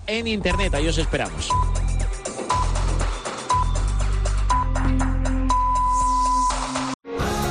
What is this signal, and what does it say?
Radio time signal (time pips), with speech. It is heard from 0.5 to 6.5 s.